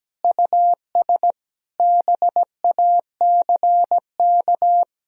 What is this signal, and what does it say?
Morse code, heard as a keyed tone, about 17 wpm, 700 Hz. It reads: US BACK
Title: Sentences from Top 1000 Words - Review Part 2 - 17wpm